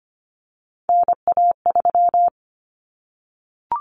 Morse code, heard as a keyed tone, about 25 wpm, 700 Hz. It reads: NA3
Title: Common Cut Numbers - 3-Digit 25wpm